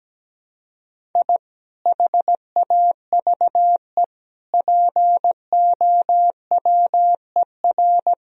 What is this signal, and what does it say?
Morse code, heard as a keyed tone, about 17 wpm, 700 Hz. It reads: I HAVE POWER
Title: Sentences from Top 1000 Words - Review Part 2 - 17wpm